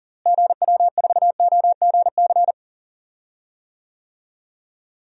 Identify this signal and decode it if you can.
Morse code, heard as a keyed tone, about 40 wpm, 700 Hz. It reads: GW4OGC
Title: DXCC-100 Call Signs - 6 Characters - Repeated 3x - 40wpm